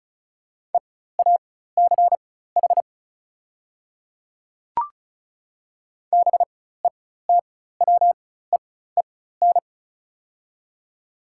Morse code, heard as a keyed tone, about 34 wpm, spaced as for 16 wpm, 700 Hz. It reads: EACH BETWEEN
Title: 4x Character Spacing - Top 300 Words - 35wpm